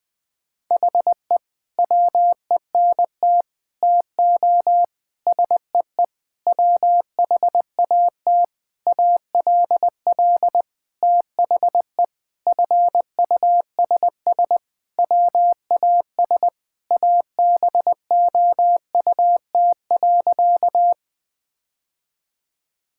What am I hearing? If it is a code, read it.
Morse code, heard as a keyed tone, about 20 wpm, 700 Hz. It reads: HE WENT TO SEE WHAT ALL THE FUSS WAS ABOUT.
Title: Sentences with 8-Word Phrases - Spoken Beginning - 20wpm